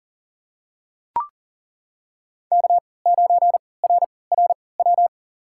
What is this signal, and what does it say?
Morse code, heard as a keyed tone, about 40 wpm, spaced as for 23 wpm, 700 Hz. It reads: K9RRW
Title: USA Call Signs - 5 Characters - 3x Character Spacing - 40wpm